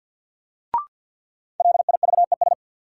Morse code, heard as a keyed tone, about 46 wpm, 700 Hz. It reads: LIVES
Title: Sets of 5 Words Encore - New Word Drill - 50wpm